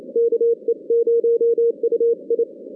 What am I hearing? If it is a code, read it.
Morse code, heard as a keyed tone, about 29 wpm, 470 Hz. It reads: KE0UI
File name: 2016 JAN CW NAQP / 2016 Jan naqp k5ot - 0-10